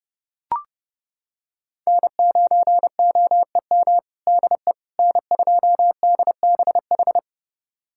Morse code, heard as a keyed tone, about 29 wpm, 700 Hz. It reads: N9OEM DE N2DBH
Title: USA Callsign DE Callsign - 5 Characters - 30wpm